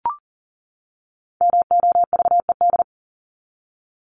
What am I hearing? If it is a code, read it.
Morse code, heard as a keyed tone, about 40 wpm, 700 Hz. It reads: MOVED
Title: ED Suffix - Top 100 Words with Base Words 40wpm